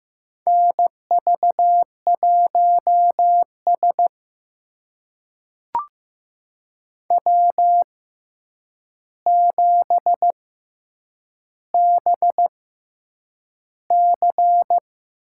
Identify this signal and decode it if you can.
Morse code, heard as a keyed tone, about 15 wpm, spaced as for 8 wpm, 700 Hz. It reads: NV1S W7BC
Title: USA Call Signs - 4 Characters - 6x Character Spacing - 15wpm